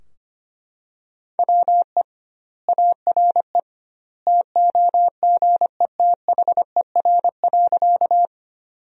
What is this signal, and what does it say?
Morse code, heard as a keyed tone, about 25 wpm, 700 Hz. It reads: WE ARE TOGETHER.
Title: 2x Word-Spacing - Sentences from Top 300 Words - 25wpm